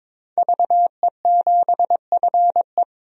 Morse code, heard as a keyed tone, about 22 wpm, 700 Hz. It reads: VE7FE
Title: DXCC-100 Call Signs - 5 Characters - 22wpm